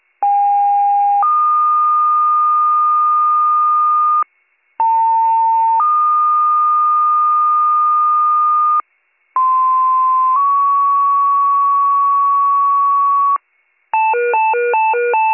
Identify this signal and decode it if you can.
Two-tone page, heard as a one-second tone, then a three-second tone, about 810 then 1220 Hz, about 900 then 1220 Hz, about 1040 then 1120 Hz. It is heard from 0.2 to 13.4 s.